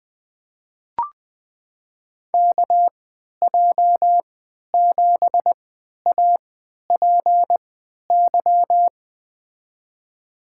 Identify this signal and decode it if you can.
Morse code, heard as a keyed tone, about 20 wpm, spaced as for 11 wpm, 700 Hz. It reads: KJ7APY
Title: USA Call Signs - 6 Characters - 3x Character Spacing - 20wpm